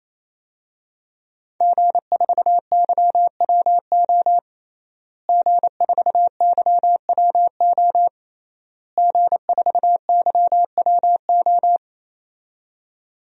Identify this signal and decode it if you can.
Morse code, heard as a keyed tone, about 28 wpm, 700 Hz. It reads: G4YWO G4YWO G4YWO
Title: DXCC-100 Call Signs - 5 Characters - Repeated 3x - 28wpm